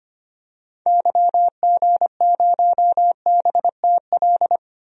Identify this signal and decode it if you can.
Morse code, heard as a keyed tone, about 25 wpm, 700 Hz. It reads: YG0BTL
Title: DXCC-100 Call Signs - 6 Characters - Repeated 2x - 25wpm